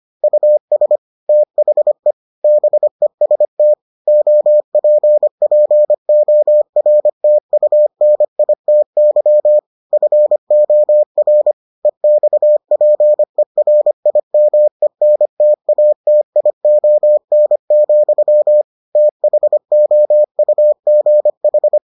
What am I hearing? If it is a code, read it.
Morse code, heard as a keyed tone, about 25 wpm, 600 Hz. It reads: US THE BEST OPPORTUNITY FOR EXPERIMENTATION, THOUGH